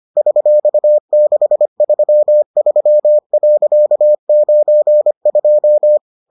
Morse code, heard as a keyed tone, about 25 wpm, 600 Hz. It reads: $633.92